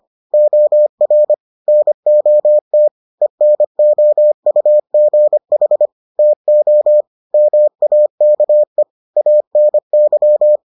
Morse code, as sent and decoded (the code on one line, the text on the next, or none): --- .-. / -. --- - / . -. --- ..- --. .... / - --- / -- .- -.- . / .- -. -.--
OR NOT ENOUGH TO MAKE ANY